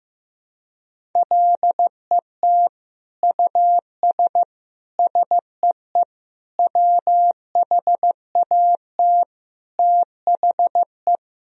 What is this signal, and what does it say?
Morse code, heard as a keyed tone, about 15 wpm, 700 Hz. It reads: LET US SEE WHAT THE
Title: Sentences from Top 1000 Words - Review Part 1 - 15wpm